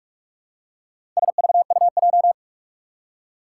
Morse code, heard as a keyed tone, about 46 wpm, 700 Hz. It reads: SVUJ
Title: Four Challenging Letter-Number Combinations 45wpm